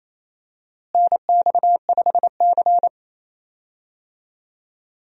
Morse code, heard as a keyed tone, about 28 wpm, 700 Hz. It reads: NX5C
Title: USA Call Signs - 4 Characters - Repeated 2x - 28wpm